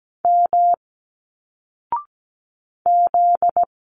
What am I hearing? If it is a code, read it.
Morse code, heard as a keyed tone, about 17 wpm, 700 Hz. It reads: M Z
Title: Single Letters 17wpm